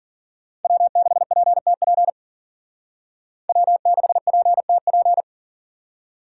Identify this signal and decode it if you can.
Morse code, heard as a keyed tone, about 54 wpm, 700 Hz. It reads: W6PTP W6PTP